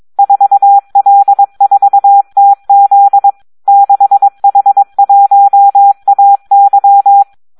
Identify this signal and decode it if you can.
Morse code, heard as a keyed tone, about 22 wpm, 800 Hz. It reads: 4L4TZ 6H1AY
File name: Audio practica CW / SD 1 22